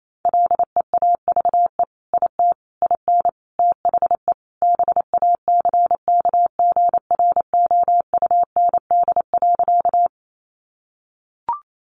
Morse code, heard as a keyed tone, about 28 wpm, 700 Hz. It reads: LEAVE IT IN THE BACKGROUND.
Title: Sentences from Top 1300 Words 28wpm